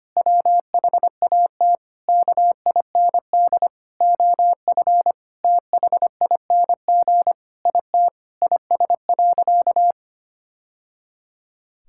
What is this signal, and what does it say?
Morse code, heard as a keyed tone, about 25 wpm, 700 Hz. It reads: WHAT KIND OF THING IT IS.
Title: Sentences from Top 1000 Words - Review Part 6 - 25wpm